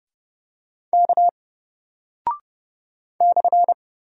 Morse code, heard as a keyed tone, about 30 wpm, 700 Hz. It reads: K /
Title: Single Character (Letter:Number:Punctuation:Prosign) Alt 30wpm